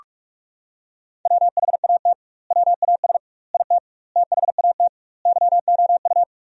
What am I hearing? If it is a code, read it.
Morse code, heard as a keyed tone, about 46 wpm, 700 Hz. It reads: WHAT WAS IT THAT YOU
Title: 2x Word-Spacing - Sentences from Top 300 Words - 45wpm